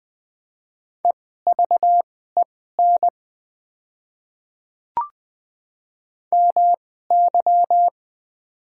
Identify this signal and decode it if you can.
Morse code, heard as a keyed tone, about 20 wpm, spaced as for 14 wpm, 700 Hz. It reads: EVEN MY
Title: 2x Character Spacing - Top 200 Words - 20wpm